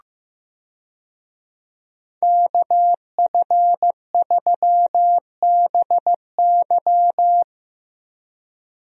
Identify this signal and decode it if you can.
Morse code, heard as a keyed tone, about 15 wpm, 700 Hz. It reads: KF3BY
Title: USA Call Signs - 5 Characters - Repeated 3x - 15wpm